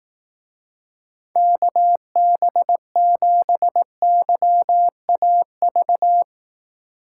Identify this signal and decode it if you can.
Morse code, heard as a keyed tone, about 18 wpm, 700 Hz. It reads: KB7YAV
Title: Speed Racing - USA 2x3 Call Signs 15wpm